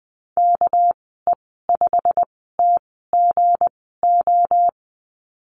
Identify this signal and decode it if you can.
Morse code, heard as a keyed tone, about 20 wpm, spaced as for 15 wpm, 700 Hz. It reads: KE5TGO
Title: 219 Lesson 35g - Number 0 - Call Signs 20wpm